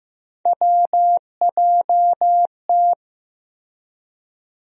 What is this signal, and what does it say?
Morse code, heard as a keyed tone, about 15 wpm, 700 Hz. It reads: WJT
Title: USA Call Sign Postfixes - 3 Letters 15wpm